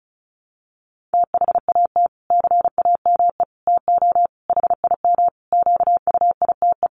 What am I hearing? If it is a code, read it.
Morse code, heard as a keyed tone, about 34 wpm, 700 Hz. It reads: THAT CAME TO HIM QUITE
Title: Sentences from Top 1300 Words 35wpm